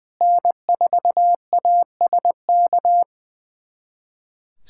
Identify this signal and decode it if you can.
Morse code, heard as a keyed tone, about 20 wpm, 690 Hz. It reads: N4ASK
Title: SOTA QSO Practice 20wpm